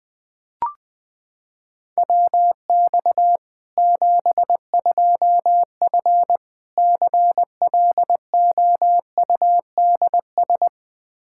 Morse code, heard as a keyed tone, about 20 wpm, 700 Hz. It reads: WX 72F CLOUDS
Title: K1USN SST QSO Practice 20wpm